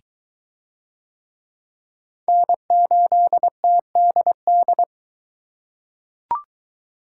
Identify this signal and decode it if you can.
Morse code, heard as a keyed tone, about 23 wpm, 700 Hz. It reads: N8TDD